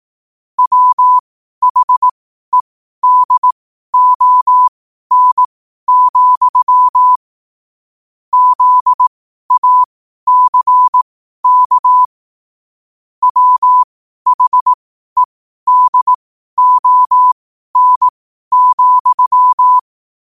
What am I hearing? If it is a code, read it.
Morse code, heard as a keyed tone, about 18 wpm, spaced as for 12 wpm, 1000 Hz. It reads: WHEDON, ZACK WHEDON,
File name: 2026-01-07-12wpm